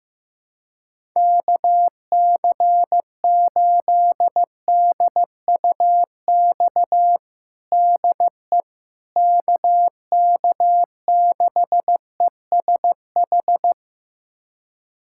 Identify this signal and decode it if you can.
Morse code, heard as a keyed tone, about 15 wpm, 700 Hz. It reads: KC8DUX DE KK6ESH
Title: USA Callsign DE Callsign 15wpm